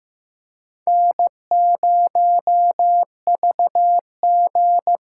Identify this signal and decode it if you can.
Morse code, heard as a keyed tone, about 15 wpm, 700 Hz. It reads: N0VG
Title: USA Call Signs - 5 Characters - 6x Character Spacing - 15wpm